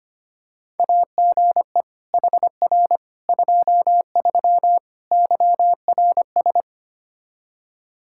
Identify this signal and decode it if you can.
Morse code, heard as a keyed tone, about 25 wpm, 700 Hz. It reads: AGE HR 23 YRS
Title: K1USN SST QSO Practice 25wpm